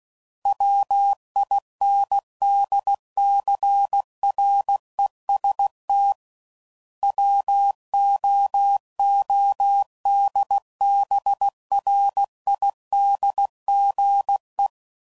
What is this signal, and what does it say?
Morse code, heard as a keyed tone, about 16 wpm, 800 Hz. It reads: WINDCREST WOODBRIDGE
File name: CW Code Groups @15WPM